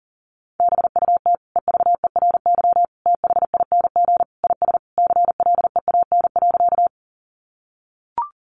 Morse code, heard as a keyed tone, about 40 wpm, 700 Hz. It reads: BUT EVERY THING IS CLEAN.
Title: Sentences from Top 1100 Words 40wpm